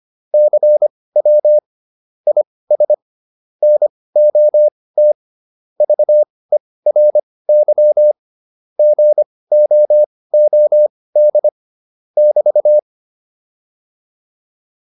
Morse code, heard as a keyed tone, about 25 wpm, spaced as for 18 wpm, 600 Hz. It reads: CW IS NOT VERY GOOD <BT>